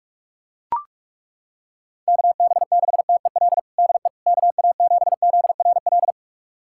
Morse code, heard as a keyed tone, about 41 wpm, 700 Hz. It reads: KB6TEL DE KA7ZRL
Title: USA Callsign DE Callsign 45wpm